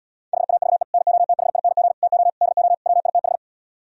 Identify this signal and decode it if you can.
Morse code, heard as a keyed tone, about 77 wpm, 700 Hz. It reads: HAVE NOTHING TO DO WITH